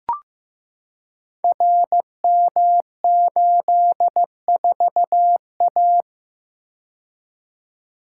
Morse code, heard as a keyed tone, about 15 wpm, 700 Hz. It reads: RM84A
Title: Code Groups - 5 Letter:Numbers - 15wpm